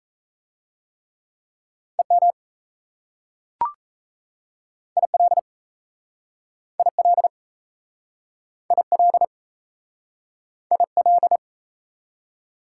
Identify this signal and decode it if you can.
Morse code, heard as a keyed tone, about 37 wpm, 700 Hz. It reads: EM IL IL IL IL